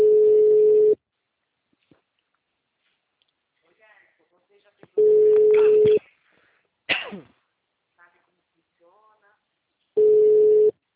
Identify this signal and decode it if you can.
Ringback tone (Europe, 425 Hz), with speech.